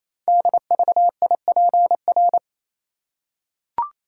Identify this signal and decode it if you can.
Morse code, heard as a keyed tone, about 28 wpm, 700 Hz. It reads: DVIPR
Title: Code Groups - 5 Letters - 28wpm